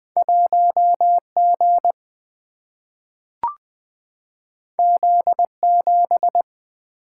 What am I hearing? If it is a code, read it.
Morse code, heard as a keyed tone, about 20 wpm, 700 Hz. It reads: 1G Z7
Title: Single-Letter Single-Number 20wpm